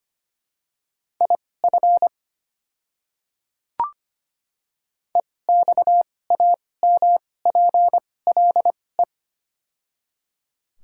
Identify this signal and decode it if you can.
Morse code, heard as a keyed tone, about 25 wpm, spaced as for 18 wpm, 700 Hz. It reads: IF EXAMPLE
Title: 2x Character Spacing - Top 500 Words - 25wpm